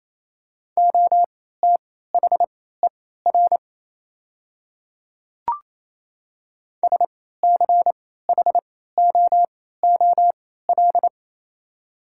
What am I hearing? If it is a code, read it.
Morse code, heard as a keyed tone, about 28 wpm, spaced as for 16 wpm, 700 Hz. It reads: OTHER SCHOOL
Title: 3x Character Spacing - Top 300 Words - 28wpm